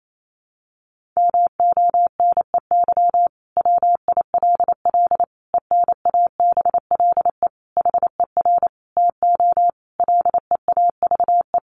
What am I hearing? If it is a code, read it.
Morse code, heard as a keyed tone, about 28 wpm, 700 Hz. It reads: MONEY WILL ENABLE HER TO LEAVE
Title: Sentences from Top 1500 Words 28wpm